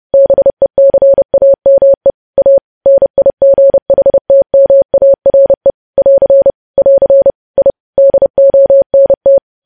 Morse code, heard as a keyed tone, about 30 wpm, 550 Hz. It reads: BECAME A NIGHTMARE <AR> <AR> I DONT
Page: news headlines in morse code at 30 wpm